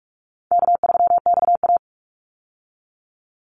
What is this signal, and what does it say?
Morse code, heard as a keyed tone, about 44 wpm, 700 Hz. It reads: K3XA
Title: Call Signs Repeated 2x - 45wpm